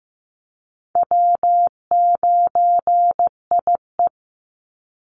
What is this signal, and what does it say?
Morse code, heard as a keyed tone, about 15 wpm, 700 Hz. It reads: W9IE
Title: Call Signs Repeated 2x - 15wpm